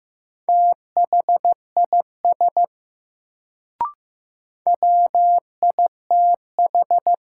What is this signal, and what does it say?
Morse code, heard as a keyed tone, about 15 wpm, 700 Hz. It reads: THIS WITH
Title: Sentences from Top 100 Words - New Word Drill - 15wpm